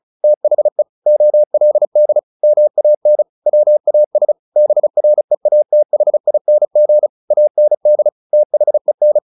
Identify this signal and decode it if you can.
Morse code, heard as a keyed tone, about 34 wpm, 600 Hz. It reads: THE OLD MAN WAS BREATHING AND THEN
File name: the old man and the sea 2 35wpm